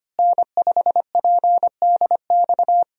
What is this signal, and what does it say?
Morse code, heard as a keyed tone, about 25 wpm, 700 Hz. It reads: N5PDX
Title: USA Call Signs - 5 Characters - Repeated 3x - 25wpm